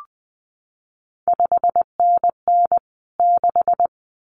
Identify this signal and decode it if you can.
Morse code, heard as a keyed tone, about 20 wpm, 700 Hz. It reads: 5NN 6
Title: CQ WW DX Contest Exchange Encore - 20wpm